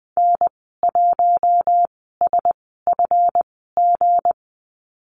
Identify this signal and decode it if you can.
Morse code, heard as a keyed tone, about 20 wpm, spaced as for 15 wpm, 700 Hz. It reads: N1SFG
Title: 179 Lesson 29g - Slash - Call Signs 20wpm